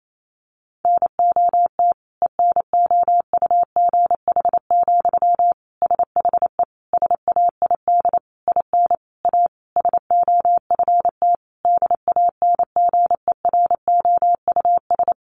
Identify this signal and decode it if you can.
Morse code, heard as a keyed tone, about 28 wpm, 700 Hz. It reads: NOT ENOUGH, SHE SAID IN A SOFT DANGEROUS
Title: Sentences from Top 1500 Words 28wpm